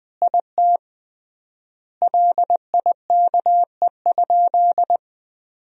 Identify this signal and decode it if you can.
Morse code, heard as a keyed tone, about 20 wpm, 700 Hz. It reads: IT LIKE?
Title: 3x Word-Spacing - Sentences from Top 100 Words - 20wpm